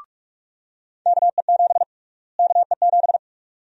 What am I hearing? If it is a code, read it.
Morse code, heard as a keyed tone, about 46 wpm, 700 Hz. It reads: KE7 KE7
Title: USA Call Sign Prefixes - Repeated 2x - 45wpm